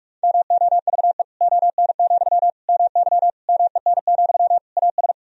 Morse code, heard as a keyed tone, about 46 wpm, 700 Hz. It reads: MOVE ON, MY MEN, AS
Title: Sentences from Top 500 Words and Abbreviations 45wpm